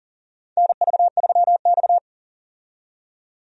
Morse code, heard as a keyed tone, about 40 wpm, 700 Hz. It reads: NV3X
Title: USA Call Signs - 4 Characters - 40wpm